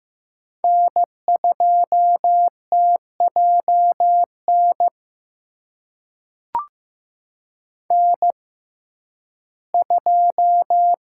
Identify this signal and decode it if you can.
Morse code, heard as a keyed tone, about 15 wpm, spaced as for 11 wpm, 700 Hz. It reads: N2TJN N2
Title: USA Call Signs - 5 Characters - 6x Character Spacing - 15wpm